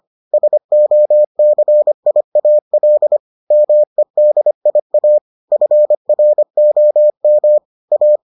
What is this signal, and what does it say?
Morse code, heard as a keyed tone, about 25 wpm, 600 Hz. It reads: SOCIAL MEDIA FROM A